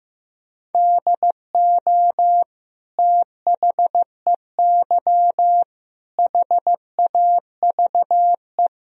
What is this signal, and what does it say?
Morse code, heard as a keyed tone, about 15 wpm, 700 Hz. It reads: DO THEY HAVE
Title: Sentences from Top 1000 Words - Review Part 4 - 15wpm